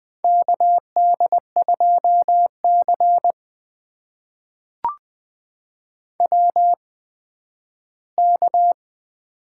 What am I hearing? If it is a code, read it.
Morse code, heard as a keyed tone, about 20 wpm, spaced as for 12 wpm, 700 Hz. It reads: KD2C WK
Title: USA Call Signs - 4 Characters - 8x Character Spacing - 20wpm